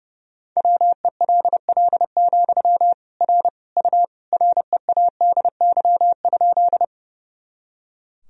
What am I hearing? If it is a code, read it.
Morse code, heard as a keyed tone, about 30 wpm, 700 Hz. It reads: WELL, R U READY?
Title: Sentences from Top 400 Words and Abbreviations 30wpm